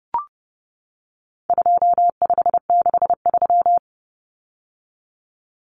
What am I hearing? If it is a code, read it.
Morse code, heard as a keyed tone, about 30 wpm, 700 Hz. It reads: 2563
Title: Common Cut Numbers - 4-Digit 30wpm